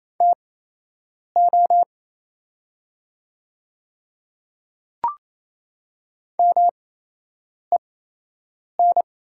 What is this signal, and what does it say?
Morse code, heard as a keyed tone, about 28 wpm, spaced as for 8 wpm, 700 Hz. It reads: TO MEN
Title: 8x Character Spacing - Top 200 Words - 28wpm